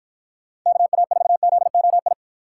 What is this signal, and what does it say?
Morse code, heard as a keyed tone, about 53 wpm, 700 Hz. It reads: KA4ZOI